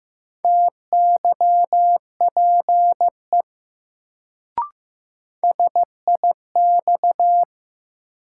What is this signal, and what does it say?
Morse code, heard as a keyed tone, about 15 wpm, 700 Hz. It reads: TYPE SIX
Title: Sentences from Top 600 Words - New Word Drill - 15wpm